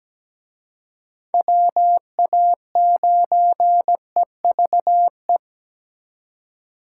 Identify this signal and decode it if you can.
Morse code, heard as a keyed tone, about 20 wpm, 700 Hz. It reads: WA9EVE
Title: Speed Racing - USA 2x3 Call Signs 17wpm